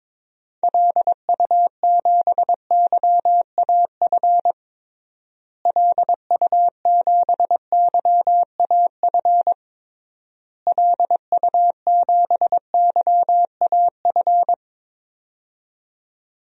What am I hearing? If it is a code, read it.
Morse code, heard as a keyed tone, about 22 wpm, 700 Hz. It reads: LU7YAF LU7YAF LU7YAF
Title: DXCC-100 Call Signs - 6 Characters - Repeated 3x - 22wpm